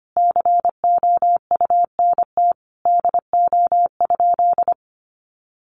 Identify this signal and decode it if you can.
Morse code, heard as a keyed tone, about 25 wpm, 700 Hz. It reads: COUNT DO?
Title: Sentences from Top 1500 Words 25wpm